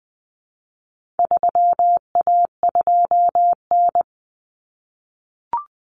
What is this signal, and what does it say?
Morse code, heard as a keyed tone, about 20 wpm, 700 Hz. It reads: 3A2N
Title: Common Cut Numbers - 4-Digit 20wpm